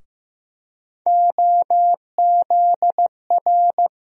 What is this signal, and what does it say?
Morse code, heard as a keyed tone, about 15 wpm, 700 Hz. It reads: OZR
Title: USA Call Sign Postfixes - 3 Letters 15wpm